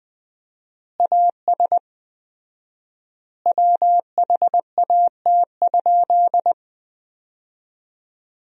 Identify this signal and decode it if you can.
Morse code, heard as a keyed tone, about 20 wpm, 700 Hz. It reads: AS WHAT?
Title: 4x Word-Spacing - Sentences from Top 200 Words - 20wpm